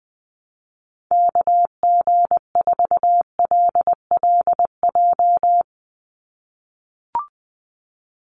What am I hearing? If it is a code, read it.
Morse code, heard as a keyed tone, about 19 wpm, 700 Hz. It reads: KG4LLJ
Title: USA Call Signs 20wpm